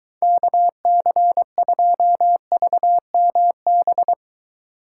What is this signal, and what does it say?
Morse code, heard as a keyed tone, about 23 wpm, 700 Hz. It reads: KC2VMB